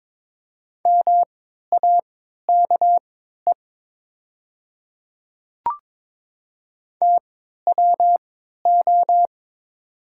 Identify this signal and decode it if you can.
Morse code, heard as a keyed tone, about 22 wpm, spaced as for 13 wpm, 700 Hz. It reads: MAKE TWO
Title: 3x Character Spacing - Top 100 Words - 22wpm